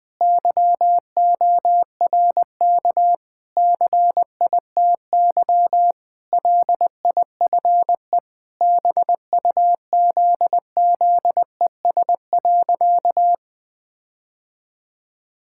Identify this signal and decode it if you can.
Morse code, heard as a keyed tone, about 19 wpm, 700 Hz. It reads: YORK CITY LIFE BUZZES.
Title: Sentences with 4-Word Phrases - 20wpm